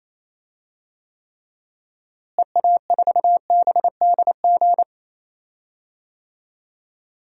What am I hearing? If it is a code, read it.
Morse code, heard as a keyed tone, about 28 wpm, 700 Hz. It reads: EA4BDG
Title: DXCC-100 Call Signs - 6 Characters - Repeated 3x - 28wpm